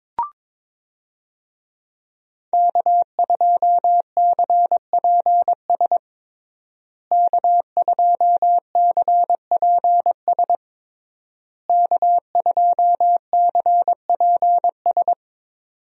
Morse code, heard as a keyed tone, about 21 wpm, 700 Hz. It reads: K2CPS K2CPS K2CPS
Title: USA Call Signs - 5 Characters - Repeated 3x - 22wpm